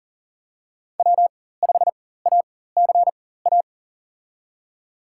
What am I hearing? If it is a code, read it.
Morse code, heard as a keyed tone, about 37 wpm, spaced as for 18 wpm, 700 Hz. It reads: W5ACA
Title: USA Call Signs - 5 Characters - 4x Character Spacing - 40wpm